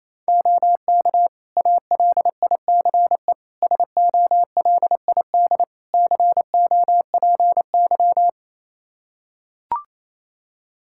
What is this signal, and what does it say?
Morse code, heard as a keyed tone, about 28 wpm, 700 Hz. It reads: OK ALICE SOLID COPY
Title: Practice QSO - Exchange 3 - More Info - 28wpm